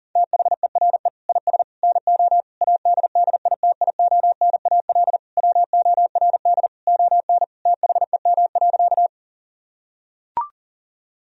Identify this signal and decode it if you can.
Morse code, heard as a keyed tone, about 40 wpm, 700 Hz. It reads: THERE IS NO ADDITIONAL WORD ON THEM.
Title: Sentences from Top 1000 Words - Review Part 4 - 40wpm